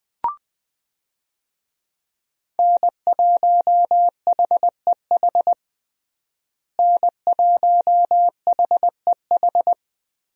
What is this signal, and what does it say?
Morse code, heard as a keyed tone, about 21 wpm, 700 Hz. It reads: N1HEH N1HEH
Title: USA Call Signs - 5 Characters - Repeated 2x - 20wpm